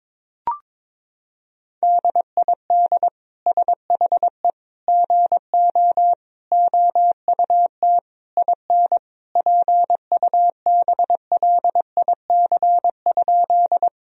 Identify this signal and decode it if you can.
Morse code, heard as a keyed tone, about 22 wpm, 700 Hz. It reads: DID SHE GO OUT IN PUBLIC?
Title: Sentences from Top 1000 Words - Review Part 3 - 22wpm